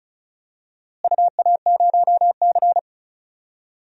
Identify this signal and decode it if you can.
Morse code, heard as a keyed tone, about 34 wpm, 690 Hz. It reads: UA0C
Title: DXCC-100 Call Signs - 4 Characters - Repeated 2x - 35wpm